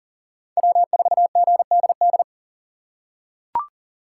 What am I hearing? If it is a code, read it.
Morse code, heard as a keyed tone, about 40 wpm, 700 Hz. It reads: W4GDD
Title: USA Call Signs - 5 Characters - 6x Character Spacing - 40wpm